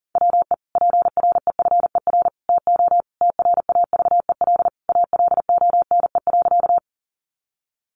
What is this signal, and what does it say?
Morse code, heard as a keyed tone, about 40 wpm, 700 Hz. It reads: WE PREFER TO TRAVEL ALONE.
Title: Sentences from Top 1100 Words 40wpm